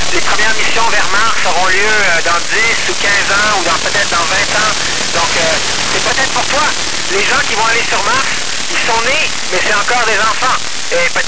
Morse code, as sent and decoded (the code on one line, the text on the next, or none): none